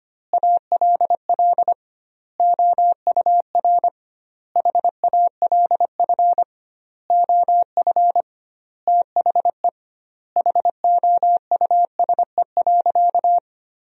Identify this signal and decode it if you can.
Morse code, heard as a keyed tone, about 25 wpm, 700 Hz. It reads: ALL OUR HALF OF THE HOUSE.
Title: 2x Word-Spacing - Sentences from Top 400 Words - 25wpm